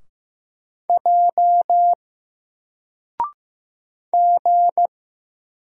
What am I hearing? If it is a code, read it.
Morse code, heard as a keyed tone, about 15 wpm, 700 Hz. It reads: J G
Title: Single Letter-Number 15wpm